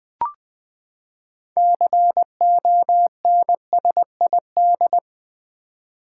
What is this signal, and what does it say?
Morse code, heard as a keyed tone, about 20 wpm, 700 Hz. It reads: CONSID
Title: Build A Word - 6 Letters At A Time - 12-Letter Words - 20wpm